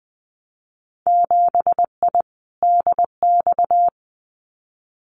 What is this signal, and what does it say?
Morse code, heard as a keyed tone, about 20 wpm, 700 Hz. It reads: 7I DX
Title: Winter Field Day Exchanges Encore 20wpm